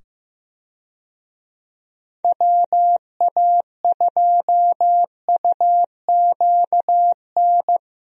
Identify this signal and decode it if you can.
Morse code, heard as a keyed tone, about 15 wpm, 700 Hz. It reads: WA2UQN
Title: USA Call Signs - 6 Characters - 15wpm